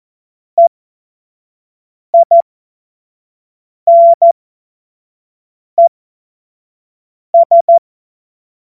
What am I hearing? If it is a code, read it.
Morse code, heard as a keyed tone, about 14 wpm, spaced as for 5 wpm, 680 Hz. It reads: EINES
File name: Morsekurs / Witze / Witz 05 Dezember 2023 25 F70